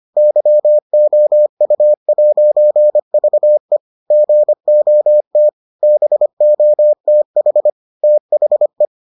Morse code, heard as a keyed tone, about 25 wpm, 600 Hz. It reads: YOU'VE GOT BOTH THE